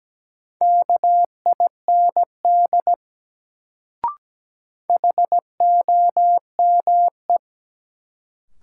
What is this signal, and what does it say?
Morse code, heard as a keyed tone, about 17 wpm, 700 Hz. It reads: KIND HOME
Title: Sentences from Top 200 Words - New Word Drill - 17wpm